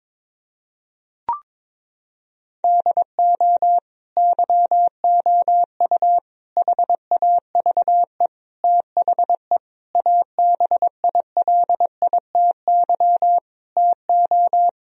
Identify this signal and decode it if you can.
Morse code, heard as a keyed tone, about 21 wpm, 700 Hz. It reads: DO YOU HAVE THE ABILITY TO
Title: Sentences from Top 1000 Words - Review Part 3 - 22wpm